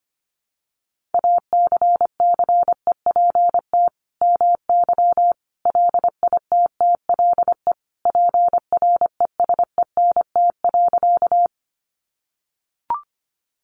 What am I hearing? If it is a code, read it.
Morse code, heard as a keyed tone, about 25 wpm, 700 Hz. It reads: ACCEPT MY LITTLE PRESENT.
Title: Sentences from Top 1300 Words 25wpm